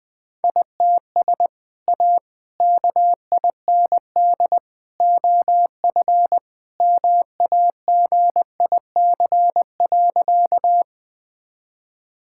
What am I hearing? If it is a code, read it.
Morse code, heard as a keyed tone, about 21 wpm, 700 Hz. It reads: ITS A KIND OF MAGIC.
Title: Sentences with 3-Word Phrases - Spoken Beginning - 20wpm